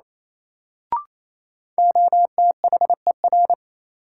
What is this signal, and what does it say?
Morse code, heard as a keyed tone, about 28 wpm, 700 Hz. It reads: OTHER
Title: Sentences from Top 100 Words - New Word Drill - 28wpm